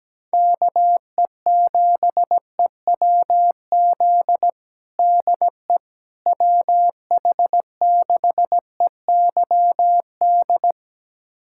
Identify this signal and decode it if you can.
Morse code, heard as a keyed tone, about 17 wpm, 700 Hz. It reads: KE7EWZ DE WH6EYD
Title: USA Callsign DE Callsign 17wpm